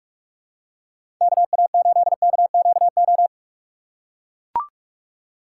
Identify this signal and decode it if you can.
Morse code, heard as a keyed tone, about 44 wpm, 700 Hz. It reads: KA8KQO
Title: USA Call Signs - 3x Character Spacing - 45wpm